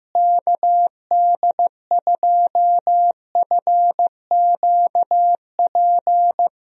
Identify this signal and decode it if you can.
Morse code, heard as a keyed tone, about 15 wpm, 700 Hz. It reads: KD2FQP
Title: USA Call Signs - 6 Characters - Repeated 3x - 15wpm